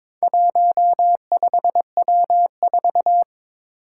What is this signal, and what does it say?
Morse code, heard as a keyed tone, about 22 wpm, 700 Hz. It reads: 15W4
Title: Four Challenging Letter-Number Combinations 22wpm